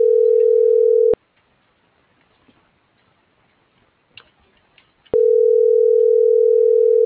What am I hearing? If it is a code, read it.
Ringback tone (North America), 440 + 480 Hz.
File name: CENTRAL GENERAL / out yarit.rdz